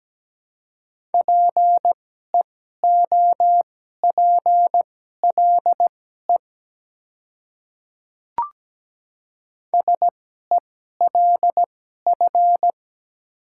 Morse code, heard as a keyed tone, about 17 wpm, spaced as for 12 wpm, 700 Hz. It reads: PEOPLE SELF
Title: 2x Character Spacing - Top 200 Words - 17wpm